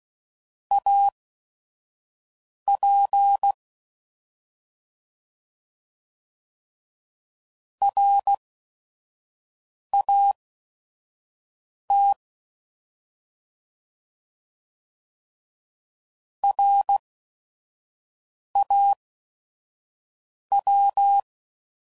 Morse code, heard as a keyed tone, about 16 wpm, spaced as for 5 wpm, 800 Hz. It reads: AP RAT RAW
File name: Prefixes / By First Letter / Rs